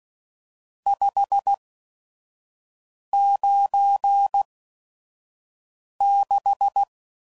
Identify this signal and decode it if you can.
Morse code, heard as a keyed tone, about 16 wpm, spaced as for 5 wpm, 800 Hz. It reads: 596
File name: CW Code Groups @5WPM / RST's / Various RST No's